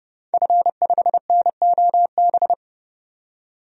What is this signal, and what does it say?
Morse code, heard as a keyed tone, about 30 wpm, 700 Hz. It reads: F5NOB
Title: DXCC-100 Call Signs - 5 Characters - 30wpm